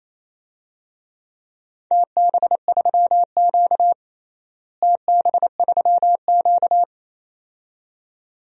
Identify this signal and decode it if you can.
Morse code, heard as a keyed tone, about 28 wpm, 700 Hz. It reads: TB3Q TB3Q
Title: DXCC-100 Call Signs - 4 Characters - Repeated 2x - 28wpm